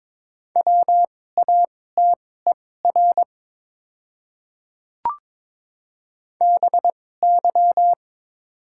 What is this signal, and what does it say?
Morse code, heard as a keyed tone, about 22 wpm, spaced as for 16 wpm, 700 Hz. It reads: WATER BY
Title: 2x Character Spacing - Top 100 Words - 22wpm